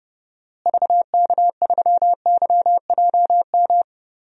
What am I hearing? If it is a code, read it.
Morse code, heard as a keyed tone, about 30 wpm, 700 Hz. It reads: VK3YJM
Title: DXCC-100 Call Signs - 6 Characters - Repeated 2x - 30wpm